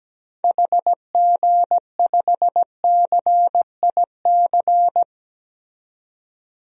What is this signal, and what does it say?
Morse code, heard as a keyed tone, about 17 wpm, 700 Hz. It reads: HG5CIC
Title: DXCC-100 Call Signs - 6 Characters - 17wpm